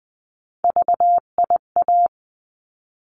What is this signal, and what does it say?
Morse code, heard as a keyed tone, about 20 wpm, 700 Hz. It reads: VIA
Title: 259 Lesson 42d - Words 18-20wpm